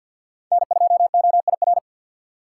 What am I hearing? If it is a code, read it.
Morse code, heard as a keyed tone, about 47 wpm, 700 Hz. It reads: N2OIR
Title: USA Call Signs - 3x Character Spacing - 50wpm